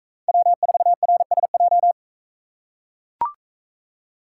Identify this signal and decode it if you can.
Morse code, heard as a keyed tone, about 43 wpm, 700 Hz. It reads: W4RSJ